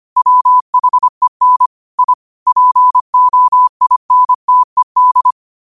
Morse code, heard as a keyed tone, about 25 wpm, 1000 Hz. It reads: WHEN I POINTED
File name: qotdfiles / 25wpm